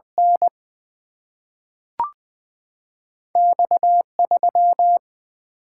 Morse code, heard as a keyed tone, about 20 wpm, 700 Hz. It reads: N X 3
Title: Single-Letter Single-Number 20wpm